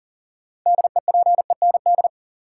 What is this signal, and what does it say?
Morse code, heard as a keyed tone, about 40 wpm, 700 Hz. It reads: DEPEND
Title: Sentences from Top 1300 Words - New Word Drill - 40wpm